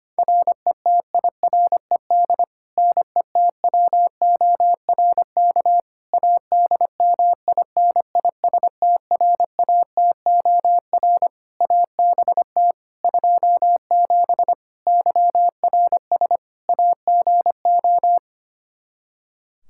Morse code, heard as a keyed tone, about 25 wpm, 700 Hz. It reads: RETIRED NETWORK ADMINISTRATOR ABT 27 YRS AGO
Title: K1USN SST QSO Practice 25wpm